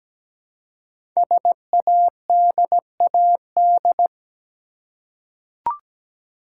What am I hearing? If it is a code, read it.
Morse code, heard as a keyed tone, about 17 wpm, 700 Hz. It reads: SADAD
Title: Code Groups - 5 Letters - 17wpm